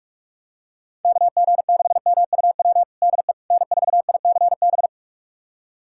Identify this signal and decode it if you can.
Morse code, heard as a keyed tone, about 46 wpm, 700 Hz. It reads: KG6MUW DE N4ICB
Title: USA Callsign DE Callsign 45wpm